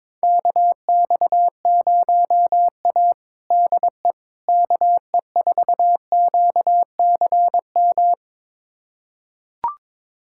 Morse code, heard as a keyed tone, about 21 wpm, 700 Hz. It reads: KX0A DE KE4QCM
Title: USA Callsign DE Callsign 22wpm